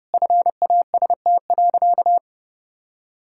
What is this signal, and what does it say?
Morse code, heard as a keyed tone, about 29 wpm, 700 Hz. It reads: FAST.
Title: Sentences from Top 1600 Words 30wpm